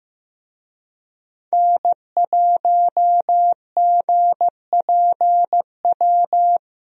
Morse code, heard as a keyed tone, about 16 wpm, 700 Hz. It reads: N1GPW
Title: USA Call Signs - 5 Characters - Repeated 2x - 15wpm